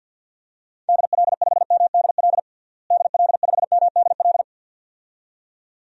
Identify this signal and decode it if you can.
Morse code, heard as a keyed tone, about 49 wpm, 700 Hz. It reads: DL5MDL DL5MDL
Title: DXCC-100 Call Signs - 6 Characters - Repeated 2x - 50wpm